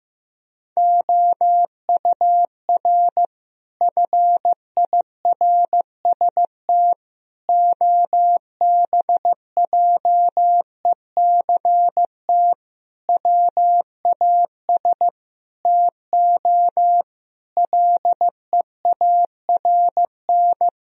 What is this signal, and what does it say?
Morse code, heard as a keyed tone, about 16 wpm, 700 Hz. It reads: OUR FIRST OBJECT WAS TO LEARN
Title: Sentences from Top 1000 Words - Review Part 3 - 15wpm